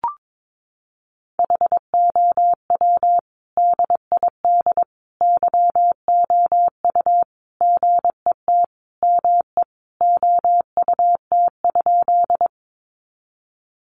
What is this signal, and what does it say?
Morse code, heard as a keyed tone, about 22 wpm, 700 Hz. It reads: HOW DID YOU GET ME OUT?
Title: Sentences from Top 200 Words 22wpm